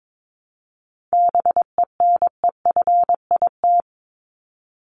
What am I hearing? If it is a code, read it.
Morse code, heard as a keyed tone, about 22 wpm, 700 Hz. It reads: BENEFIT
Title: Top 751-1000 Words 22wpm